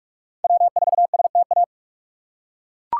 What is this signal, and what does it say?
Morse code, heard as a keyed tone, about 41 wpm, 700 Hz. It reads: W4STA
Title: USA Call Signs - 5 Characters - 6x Character Spacing - 45wpm